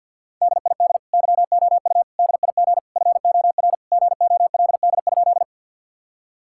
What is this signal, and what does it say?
Morse code, heard as a keyed tone, about 50 wpm, 700 Hz. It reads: DID YOU DIG FOR GOLD?
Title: Sentences from Top 1600 Words 50wpm